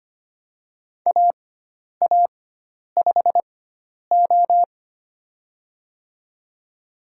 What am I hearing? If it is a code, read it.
Morse code, heard as a keyed tone, about 25 wpm, spaced as for 10 wpm, 700 Hz. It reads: AA5O
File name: Lesson 16g - Letter c - Call Signs 25wpm